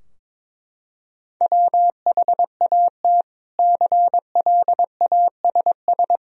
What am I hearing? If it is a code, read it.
Morse code, heard as a keyed tone, about 22 wpm, 700 Hz. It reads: WHAT CLASS
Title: Sentences from Top 1000 Words - Review Part 2 - 22wpm